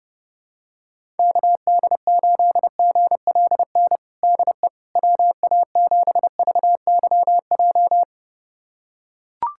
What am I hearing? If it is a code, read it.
Morse code, heard as a keyed tone, about 29 wpm, 700 Hz. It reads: KD8GLN DE WA7VYJ
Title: USA Callsign DE Callsign 30wpm